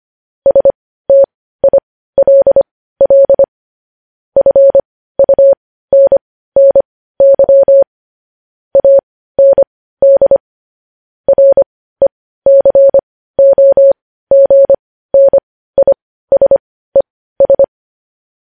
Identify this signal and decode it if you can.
Morse code, heard as a keyed tone, about 25 wpm, spaced as for 15 wpm, 550 Hz. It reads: STILL FUNNY AND RECOGNISES
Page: News Headlines in Morse Code at 15 WPM